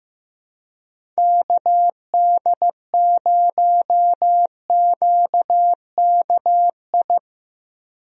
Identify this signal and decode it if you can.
Morse code, heard as a keyed tone, about 15 wpm, 700 Hz. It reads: KD0QKI
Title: USA Call Signs - 6 Characters - Repeated 2x - 15wpm